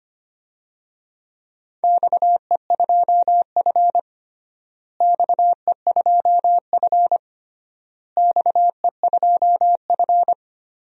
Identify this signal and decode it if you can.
Morse code, heard as a keyed tone, about 25 wpm, 700 Hz. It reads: XE2F XE2F XE2F
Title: DXCC-100 Call Signs - 4 Characters - Repeated 3x - 25wpm